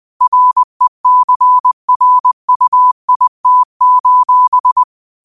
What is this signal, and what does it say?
Morse code, heard as a keyed tone, about 20 wpm, 1000 Hz. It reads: RECRUIT:
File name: qotdfiles / 20wpm